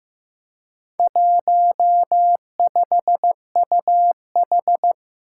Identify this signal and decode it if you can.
Morse code, heard as a keyed tone, about 15 wpm, 700 Hz. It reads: 15UH
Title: Four Challenging Letter-Number Combinations 15wpm